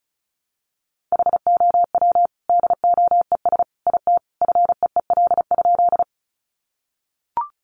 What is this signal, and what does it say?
Morse code, heard as a keyed tone, about 34 wpm, 700 Hz. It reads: HOW DOES IT FEEL?
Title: Sentences from Top 400 Words 35wpm